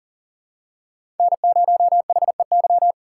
Morse code, heard as a keyed tone, about 40 wpm, 700 Hz. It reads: N0HEY
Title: USA Call Signs - 5 Characters - 4x Character Spacing - 40wpm